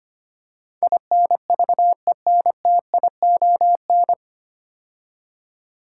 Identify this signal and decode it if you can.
Morse code, heard as a keyed tone, about 25 wpm, 700 Hz. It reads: INVENTION
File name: practice / 25wpm / 9 Letter Words - First 3 Letters Given - 25wpm